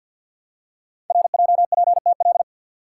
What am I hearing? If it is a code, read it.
Morse code, heard as a keyed tone, about 48 wpm, 700 Hz. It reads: RJPTL
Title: Code Groups - 5 Letters - 50wpm